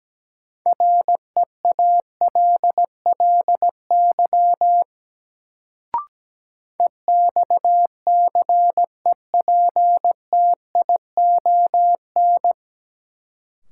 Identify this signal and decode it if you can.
Morse code, heard as a keyed tone, about 17 wpm, 700 Hz. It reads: REALLY EXCEPTION
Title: Sets of 4 Words - New Word Drill - 17wpm